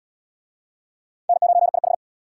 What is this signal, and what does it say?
Morse code, heard as a keyed tone, about 64 wpm, 700 Hz. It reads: N9IH